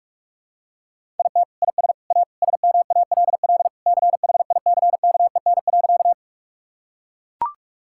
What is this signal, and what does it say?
Morse code, heard as a keyed tone, about 44 wpm, 690 Hz. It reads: IT IS A SMALL CHICKEN.
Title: Sentences from Top 1000 Words - Review Part 5 - 45wpm